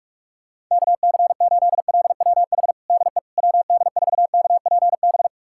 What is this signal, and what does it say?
Morse code, heard as a keyed tone, about 41 wpm, 700 Hz. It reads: KC8LWH DE WD4KPB
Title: USA Callsign DE Callsign 45wpm